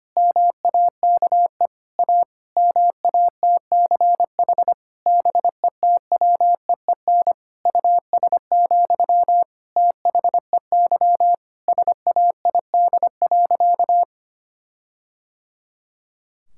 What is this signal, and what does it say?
Morse code, heard as a keyed tone, about 25 wpm, 700 Hz. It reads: MAKE A MATCH BETWEEN US, THEY SAID.
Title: Sentences from Top 1600 Words 25wpm